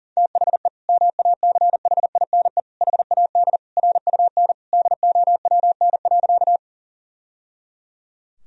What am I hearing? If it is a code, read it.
Morse code, heard as a keyed tone, about 37 wpm, 700 Hz. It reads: THE MACHINE HAD RUN DOWN.
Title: Sentences from Top 1000 Words - Review Part 2 - 40wpm